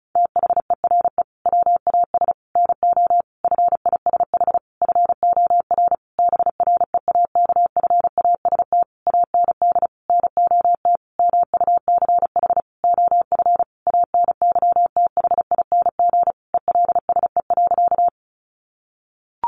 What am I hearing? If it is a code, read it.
Morse code, heard as a keyed tone, about 34 wpm, 700 Hz. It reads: THERE WAS NO FISH FOR BREAKFAST AND NOT MUCH OF ANYTHING ELSE.
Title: Sentences from Top 1500 Words 35wpm